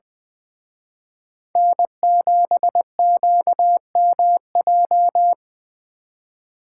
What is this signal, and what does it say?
Morse code, heard as a keyed tone, about 24 wpm, 700 Hz. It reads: N7QMJ
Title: Speed Racing - USA 1x3 Call Signs 20wpm